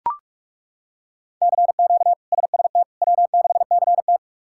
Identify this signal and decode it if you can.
Morse code, heard as a keyed tone, about 42 wpm, 700 Hz. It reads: CQ SST W6CT
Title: K1USN SST QSO Practice - 45wpm